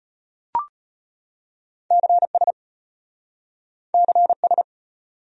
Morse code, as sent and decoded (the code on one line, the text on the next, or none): -.-. ... / -.-. ...
CS CS